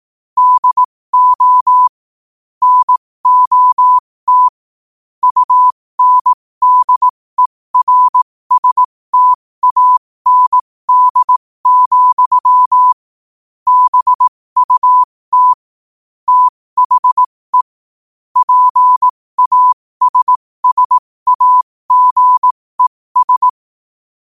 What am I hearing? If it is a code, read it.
Morse code, heard as a keyed tone, about 18 wpm, spaced as for 15 wpm, 1000 Hz. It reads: DO NOT UNDERSTAND, BUT THE PASSAGES
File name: qotdfiles / 15wpm